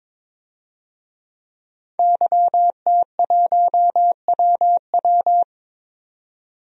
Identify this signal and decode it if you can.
Morse code, heard as a keyed tone, about 22 wpm, 700 Hz. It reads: YT1WW
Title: DXCC-100 Call Signs - 5 Characters - 22wpm